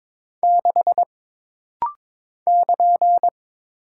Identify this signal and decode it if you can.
Morse code, heard as a keyed tone, about 22 wpm, 700 Hz. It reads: 6 <KN>
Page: This rapid-fire timing is geared to push your ICR (Instant Character Recognition) skills.